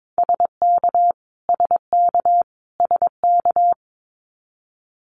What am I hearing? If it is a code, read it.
Morse code, heard as a keyed tone, about 22 wpm, 700 Hz. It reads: SK SK SK
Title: Canadian Provinces and Territories - Repeated 3x - 22wpm